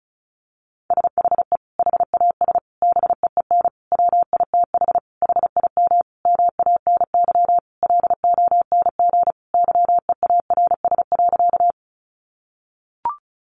Morse code, heard as a keyed tone, about 34 wpm, 700 Hz. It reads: SHE HAS BEEN WITH HIM MANY LONG YEARS.
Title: Sentences from Top 200 Words 35wpm